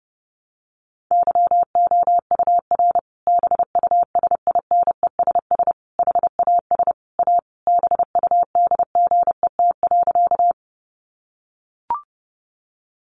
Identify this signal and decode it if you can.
Morse code, heard as a keyed tone, about 30 wpm, 700 Hz. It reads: YOUR BUSINESS HAS A BUDGET.
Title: Sentences from Top 1200 Words 30wpm